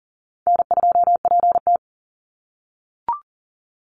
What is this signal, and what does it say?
Morse code, heard as a keyed tone, about 40 wpm, 700 Hz. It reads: N2PT
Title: USA Call Signs 40wpm